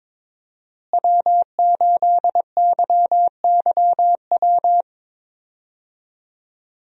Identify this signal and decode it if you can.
Morse code, heard as a keyed tone, about 22 wpm, 690 Hz. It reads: W8YYW
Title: USA Call Signs - 5 Characters - Repeated 3x - 22wpm